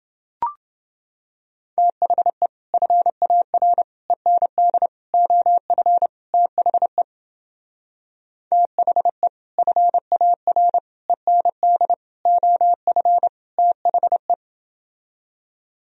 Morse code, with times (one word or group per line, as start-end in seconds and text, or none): 1.8-2.5 THE
2.7-3.8 FAR
4.1-4.9 END
5.1-6.1 OF
6.3-7.0 THE
8.5-9.3 THE
9.6-10.8 FAR
11.1-11.9 END
12.3-13.3 OF
13.6-14.3 THE